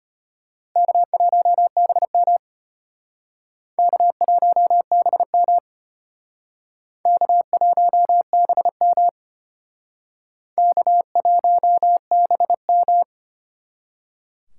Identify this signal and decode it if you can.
Morse code, heard as a keyed tone, about 32 wpm, 700 Hz. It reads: K1BM K1BM K1BM K1BM